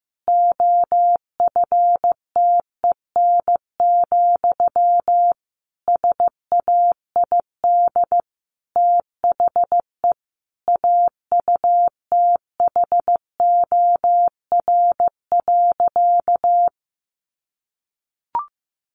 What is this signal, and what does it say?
Morse code, heard as a keyed tone, about 15 wpm, 700 Hz. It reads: OFTEN, SAID THE AUTHOR.
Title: Sentences from Top 1200 Words 15wpm